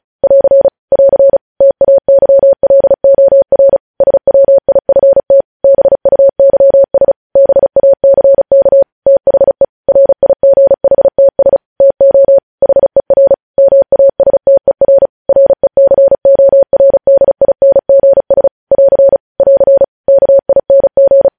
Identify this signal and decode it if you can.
Morse code, heard as a keyed tone, about 36 wpm, 550 Hz. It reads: <AR> <AR> TAYLOR SWIFT BUYS BACK THE RIGHTS TO HER MASTER RECORDINGS <AR> <AR> KING
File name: podcast / News headlines-1x-35x35